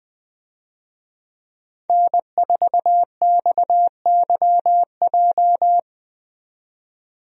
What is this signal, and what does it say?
Morse code, heard as a keyed tone, about 20 wpm, 700 Hz. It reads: N4XYJ
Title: USA Call Signs - 5 Characters - 20wpm